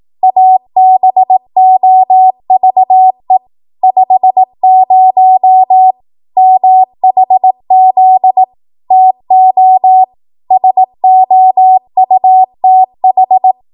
Morse code, heard as a keyed tone, about 18 wpm, 750 Hz. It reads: ABOVE 50 MHZ TO SOUTH
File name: Archive / 18 WPM / 18WPM